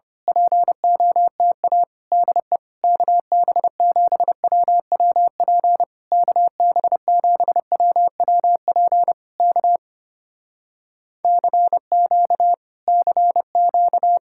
Morse code, heard as a keyed tone, about 26 wpm, 700 Hz. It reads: POTA DE KB7WWP KB7WWP K CQ CQ
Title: Speed-Racing Common QSO Phrases - 25wpm